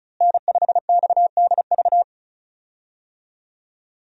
Morse code, heard as a keyed tone, about 34 wpm, 700 Hz. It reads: N5XDV
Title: USA Call Signs - 5 Characters - Repeated 2x - 35wpm